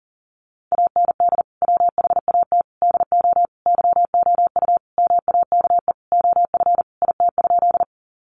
Morse code, heard as a keyed tone, about 40 wpm, 700 Hz. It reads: AND WHAT DO YOU MAKE OF IT?
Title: Sentences from Top 100 Words 40wpm